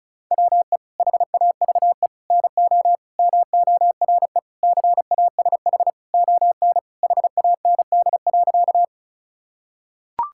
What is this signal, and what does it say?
Morse code, heard as a keyed tone, about 34 wpm, 690 Hz. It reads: WE HAVE NO MORE CASH ON HAND.
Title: Sentences from Top 1000 Words - Review Part 5 - 35wpm